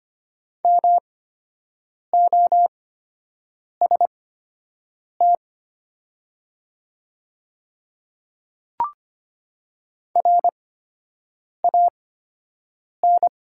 Morse code, heard as a keyed tone, about 25 wpm, spaced as for 7 wpm, 700 Hz. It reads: MOST RAN
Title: 8x Character Spacing - Top 500 Words - 25wpm